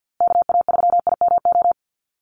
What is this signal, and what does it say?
Morse code, heard as a keyed tone, about 50 wpm, 700 Hz. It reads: KA3IMO
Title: USA Call Signs Repeated 3x 50wpm